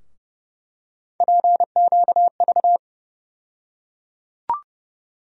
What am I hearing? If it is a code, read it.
Morse code, heard as a keyed tone, about 29 wpm, 700 Hz. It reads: PQV
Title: USA Call Sign Postfixes - 3 Letters 30wpm